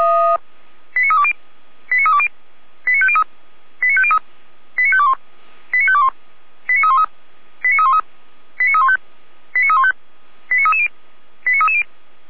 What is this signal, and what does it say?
Five-tone selective call (ZVEI), heard as quick runs of five tones, tones of about 70 ms. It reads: 89320 89320 89503 89503 89521 89521 89313 89313 89315 89315 89330 89330
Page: ZVEI